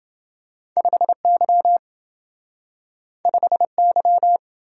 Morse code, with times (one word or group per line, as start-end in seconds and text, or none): 0.8-1.8 5Y
3.3-4.4 5Y